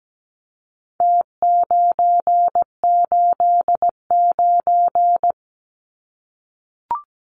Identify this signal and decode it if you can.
Morse code, heard as a keyed tone, about 17 wpm, 700 Hz. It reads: T989
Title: Numbers - Contest Style - 4-Digit 17wpm